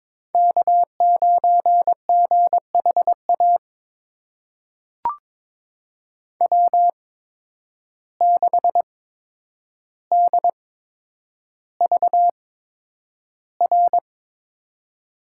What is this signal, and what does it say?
Morse code, heard as a keyed tone, about 22 wpm, spaced as for 9 wpm, 700 Hz. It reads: K9GHA W6DVR
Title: USA Call Signs - 5 Characters - 8x Character Spacing - 22wpm